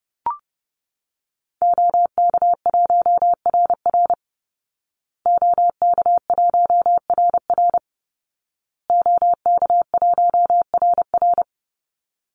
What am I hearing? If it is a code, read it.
Morse code, heard as a keyed tone, about 28 wpm, 700 Hz. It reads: OK1RR OK1RR OK1RR
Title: Call Signs Repeated 2x - 30wpm